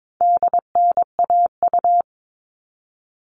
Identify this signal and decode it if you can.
Morse code, heard as a keyed tone, about 22 wpm, 700 Hz. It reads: DNAU
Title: All Cut Numbers - 4-Digit 22wpm